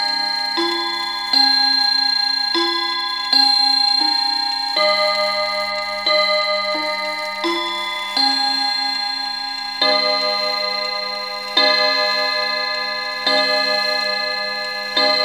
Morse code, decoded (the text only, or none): none